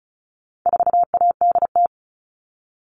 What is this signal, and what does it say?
Morse code, heard as a keyed tone, about 34 wpm, 700 Hz. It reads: 4ADT
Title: All Cut Numbers - 4-Digit 35wpm